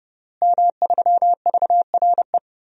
Morse code, heard as a keyed tone, about 32 wpm, 700 Hz. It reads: M3VRE
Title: DXCC-100 Call Signs - 5 Characters - Repeated 3x - 30wpm